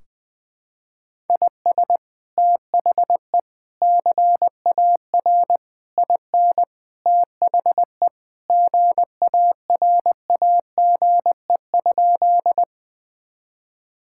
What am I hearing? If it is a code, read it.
Morse code, heard as a keyed tone, about 21 wpm, 700 Hz. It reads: IS THE CAR IN THE GARAGE?
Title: Sentences from Top 1600 Words 20wpm